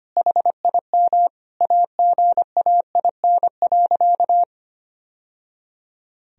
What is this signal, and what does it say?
Morse code, heard as a keyed tone, about 25 wpm, 700 Hz. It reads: HIM AGAIN.
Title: Sentences from Top 1000 Words - Review Part 1 - 25wpm